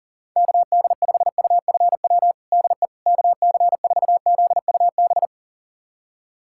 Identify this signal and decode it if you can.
Morse code, heard as a keyed tone, about 40 wpm, 700 Hz. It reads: KD5UFW DE KC4ZUB
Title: USA Callsign DE Callsign - 6 Characters - 40wpm